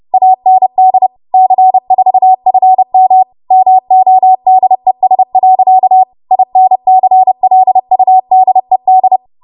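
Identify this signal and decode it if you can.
Morse code, heard as a keyed tone, about 30 wpm, 750 Hz. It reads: AND C4FM MODES. INCLUDED
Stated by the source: Morse practice transmission at 30 WPM.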